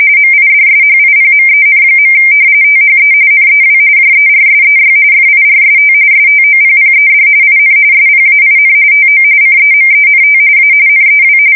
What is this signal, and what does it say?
RTTY, about 45.45 baud. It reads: MIDSHIPMEN. THE NEW JERSEY FIRST STOPPED IN SPAIN TO CEMENT A NEW
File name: fsk170 / battleshipnj